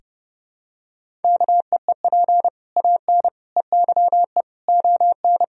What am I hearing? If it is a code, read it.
Morse code, heard as a keyed tone, about 30 wpm, 700 Hz. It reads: KEEP AN EYE ON
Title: Sentences from Top 1000 Words - Review Part 1 - 30wpm